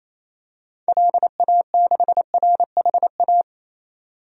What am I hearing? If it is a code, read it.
Morse code, heard as a keyed tone, about 27 wpm, 700 Hz. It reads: LA6RHA
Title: DXCC-100 Call Signs - 6 Characters - 28wpm